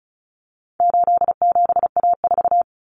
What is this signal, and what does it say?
Morse code, heard as a keyed tone, about 34 wpm, 700 Hz. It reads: 87A4
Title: Common Cut Numbers - 4-Digit 35wpm